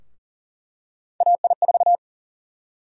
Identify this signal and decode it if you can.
Morse code, heard as a keyed tone, about 40 wpm, 700 Hz. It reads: AI4
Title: USA Call Sign Prefixes - 40wpm